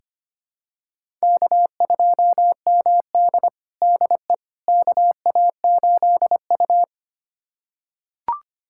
Morse code, heard as a keyed tone, about 25 wpm, 700 Hz. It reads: K2MD DE KA8U
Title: USA Callsign DE Callsign - 4 Characters - 25wpm